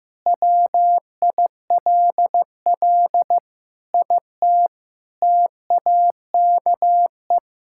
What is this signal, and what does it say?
Morse code, heard as a keyed tone, about 15 wpm, 700 Hz. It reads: WILL IT TAKE
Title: Sentences from Top 1000 Words - Review Part 5 - 15wpm